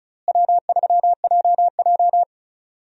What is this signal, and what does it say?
Morse code, heard as a keyed tone, about 36 wpm, 700 Hz. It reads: W3JJ
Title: USA Call Signs - 4 Characters - Repeated 3x - 35wpm